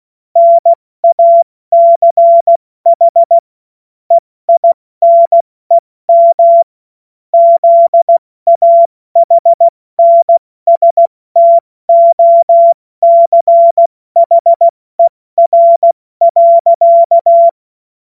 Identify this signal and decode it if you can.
Morse code, heard as a keyed tone, about 16 wpm, spaced as for 14 wpm, 680 Hz. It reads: NACH EINEM ZAHNSTOCHER.
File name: Morsekurs / Witze / Witz 02 Januar 2024 70 F80